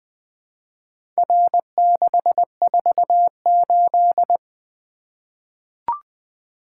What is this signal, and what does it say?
Morse code, heard as a keyed tone, about 20 wpm, 700 Hz. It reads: R648
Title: Four Letter-Number Combinations - 20wpm